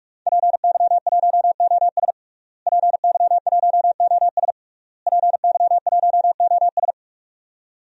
Morse code, heard as a keyed tone, about 41 wpm, 700 Hz. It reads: PY1OS PY1OS PY1OS
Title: DXCC-100 Call Signs - 5 Characters - Repeated 3x - 45wpm